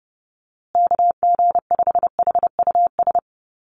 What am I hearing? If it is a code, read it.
Morse code, heard as a keyed tone, about 30 wpm, 700 Hz. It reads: KG5HUS
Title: USA Call Signs Repeated 3x 30wpm